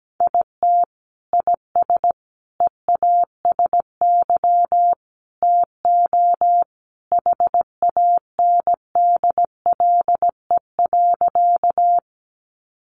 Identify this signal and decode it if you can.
Morse code, heard as a keyed tone, about 17 wpm, 700 Hz. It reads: IT IS EASY TO HANDLE.
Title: Sentences from Top 1100 Words 17wpm